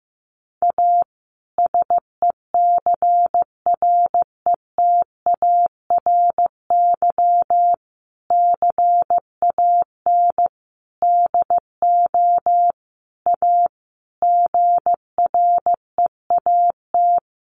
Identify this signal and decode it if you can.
Morse code, heard as a keyed tone, about 15 wpm, 700 Hz. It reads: A SECRETARY CAN DO A GREAT
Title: Sentences from Top 1300 Words 15wpm